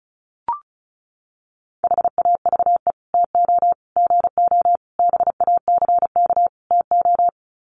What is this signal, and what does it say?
Morse code, heard as a keyed tone, about 34 wpm, 700 Hz. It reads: HAVE TO GO BACK TO
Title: Sets of 5 Words Encore2 35wpm